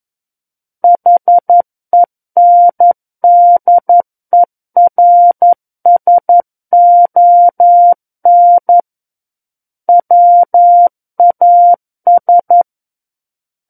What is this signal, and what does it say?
Morse code, heard as a keyed tone, about 11 wpm, 700 Hz. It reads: HENDERSON WAS
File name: 11 WPM